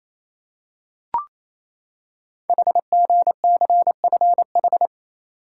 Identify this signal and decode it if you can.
Morse code, heard as a keyed tone, about 27 wpm, 700 Hz. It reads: HGCFH
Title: Code Groups - 5 Letters - 28wpm